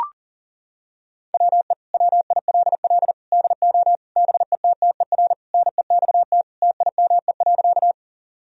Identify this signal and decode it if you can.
Morse code, heard as a keyed tone, about 40 wpm, 700 Hz. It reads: WE WILL DO BETTER NEXT TIME.
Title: Sentences from Top 1000 Words - Review Part 1 - 40wpm